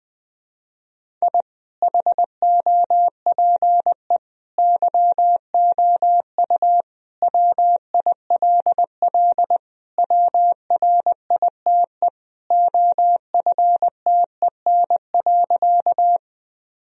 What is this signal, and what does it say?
Morse code, heard as a keyed tone, about 20 wpm, 700 Hz. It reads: I HOPE YOU WILL WRITE OFTEN.
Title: Sentences from Top 1000 Words - Review Part 1 - 20wpm